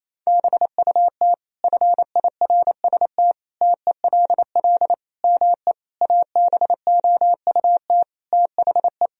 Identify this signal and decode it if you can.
Morse code, heard as a keyed tone, about 27 wpm, 700 Hz. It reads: BUT FIRST TELL ME ABOUT THE
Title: Sentences from Top 1000 Words - Review Part 2 - 28wpm